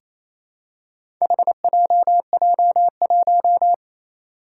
Morse code, heard as a keyed tone, about 28 wpm, 700 Hz. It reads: HJJ1
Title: Four Challenging Letter-Number Combinations 28wpm